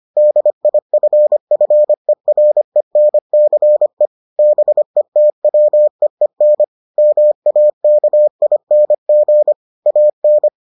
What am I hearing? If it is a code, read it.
Morse code, heard as a keyed tone, about 25 wpm, 600 Hz. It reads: DIFFERENCE BETWEEN MAKING AN